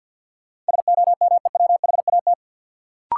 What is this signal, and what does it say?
Morse code, heard as a keyed tone, about 48 wpm, 700 Hz. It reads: SOMEWHAT
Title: Sentences from Top 1300 Words - New Word Drill Alt - 50wpm